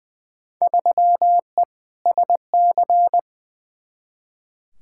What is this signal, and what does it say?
Morse code, heard as a keyed tone, about 20 wpm, 700 Hz. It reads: 3E SC
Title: Field Day Exchanges Encore - 20wpm